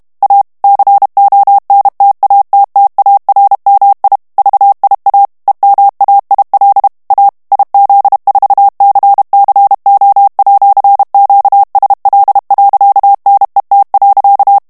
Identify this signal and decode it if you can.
Morse code, heard as a keyed tone, about 32 wpm, 800 Hz. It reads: A CONTATTARMI VIA EMAIL A IZ4CCO@QSL.NET.
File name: ITALIEN texte clair 1 vitesse 30 mots minute